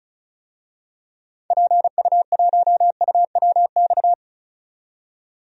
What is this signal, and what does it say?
Morse code, heard as a keyed tone, about 36 wpm, 700 Hz. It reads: PU1UWX
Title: DXCC-100 Call Signs - 6 Characters - 35wpm